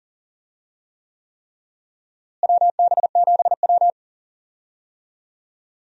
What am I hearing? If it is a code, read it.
Morse code, heard as a keyed tone, about 40 wpm, 700 Hz. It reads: WB7W
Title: USA Call Signs - 4 Characters - Repeated 3x - 40wpm